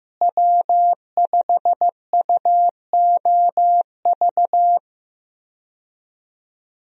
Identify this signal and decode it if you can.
Morse code, heard as a keyed tone, about 15 wpm, 700 Hz. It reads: W5UOV
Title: USA Call Signs - 5 Characters - Repeated 2x - 15wpm